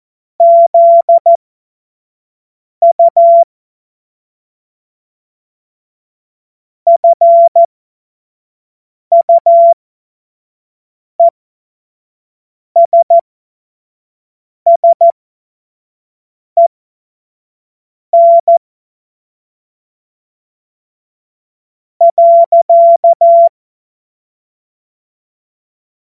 Morse code, heard as a keyed tone, about 14 wpm, spaced as for 5 wpm, 680 Hz. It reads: ZU FUESSEN .
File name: Morsekurs / Witze / Witz 04. Oktober 2022 25 F70